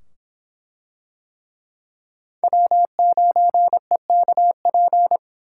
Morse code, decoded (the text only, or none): W9EKP